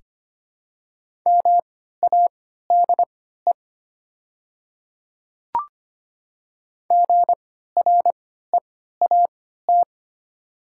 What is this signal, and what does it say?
Morse code, heard as a keyed tone, about 25 wpm, spaced as for 14 wpm, 690 Hz. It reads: MADE GREAT
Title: 3x Character Spacing - Top 200 Words - 25wpm